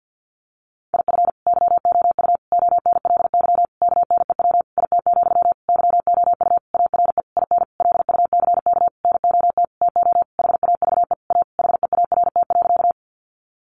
Morse code, heard as a keyed tone, about 48 wpm, 700 Hz. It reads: IF YOU ONLY KNEW IT, YOU ARE IN LUCK NOT TO HAVE A HEART.
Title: Sentences from Top 1500 Words 50wpm